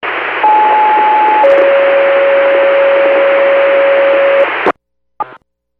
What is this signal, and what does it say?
Two-tone page, heard as a one-second tone, then a three-second tone, about 860 then 550 Hz, with noise. It is heard from 0.4 to 4.4 s.